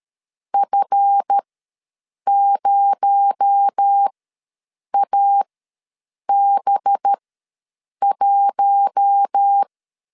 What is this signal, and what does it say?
Morse code, heard as a keyed tone, about 13 wpm, spaced as for 7 wpm, 800 Hz. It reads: F0AB1
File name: LEZIONI / 7 GF9Z / 7 GF9 ws 12wpm 3344